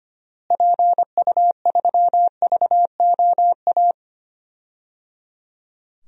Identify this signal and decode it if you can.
Morse code, heard as a keyed tone, about 25 wpm, 700 Hz. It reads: PU3VOA
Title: DXCC-100 Call Signs - 6 Characters - Repeated 3x - 25wpm